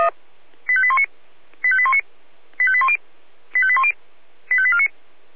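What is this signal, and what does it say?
Five-tone selective call (ZVEI), heard as quick runs of five tones, tones of about 70 ms. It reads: 85819 85819 85810 85810 85849
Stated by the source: ZVEI five-tone selective call received on an SDR.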